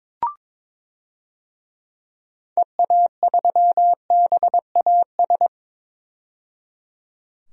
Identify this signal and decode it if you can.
Morse code, heard as a keyed tone, about 22 wpm, 700 Hz. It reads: EA3BAS
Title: DXCC-100 Call Signs - 6 Characters - 22wpm